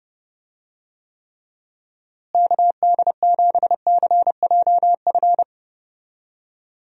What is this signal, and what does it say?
Morse code, heard as a keyed tone, about 30 wpm, 700 Hz. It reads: KD7CJF
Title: USA Call Signs - 6 Characters - Repeated 3x - 30wpm